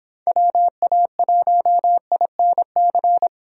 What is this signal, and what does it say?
Morse code, heard as a keyed tone, about 25 wpm, 700 Hz. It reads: WA1INC